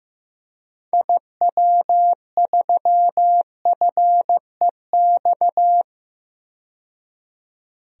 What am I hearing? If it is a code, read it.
Morse code, heard as a keyed tone, about 15 wpm, 700 Hz. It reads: IW3FEX
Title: DXCC-100 Call Signs - 6 Characters - 15wpm